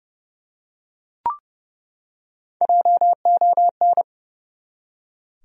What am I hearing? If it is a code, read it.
Morse code, heard as a keyed tone, about 30 wpm, 700 Hz. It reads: JON
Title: USA Common Names - Max 4 Letters - 30wpm